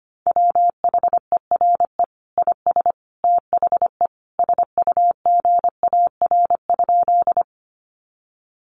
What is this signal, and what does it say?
Morse code, heard as a keyed tone, about 25 wpm, 700 Hz. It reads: WHERE IS THE SUGAR?
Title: Sentences from Top 1200 Words 25wpm